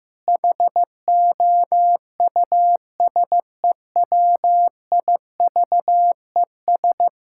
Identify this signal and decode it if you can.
Morse code, heard as a keyed tone, about 16 wpm, 700 Hz. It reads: HOUSEWIVES
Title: Pangram Challenge v2 - 15wpm